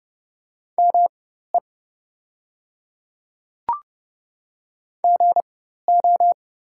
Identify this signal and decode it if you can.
Morse code, heard as a keyed tone, about 30 wpm, spaced as for 14 wpm, 700 Hz. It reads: ME GO
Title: 4x Character Spacing - Top 300 Words - 30wpm